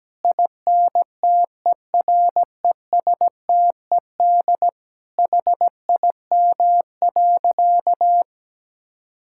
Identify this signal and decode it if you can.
Morse code, heard as a keyed tone, about 17 wpm, 700 Hz. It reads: INTERESTED HIM.
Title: Sentences from Top 1000 Words - Review Part 6 - 17wpm